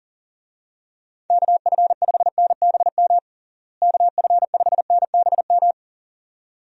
Morse code, heard as a keyed tone, about 40 wpm, 700 Hz. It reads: KF5NBM KF5NBM
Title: USA Call Signs - 6 Characters - Repeated 2x - 40wpm